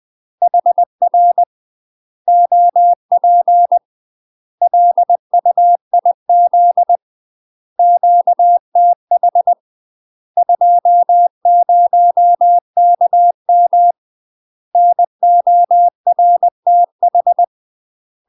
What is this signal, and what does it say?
Morse code, heard as a keyed tone, about 20 wpm, 700 Hz. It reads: HR OP LUIZ QTH 20KM NORTH
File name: UK MASTER QSOs TRAINING / 20WPM / UK MASTER QSOs 0036